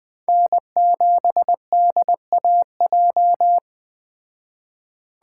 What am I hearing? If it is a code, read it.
Morse code, heard as a keyed tone, about 20 wpm, 700 Hz. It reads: N7DAJ
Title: Speed Racing - USA Call Signs 17wpm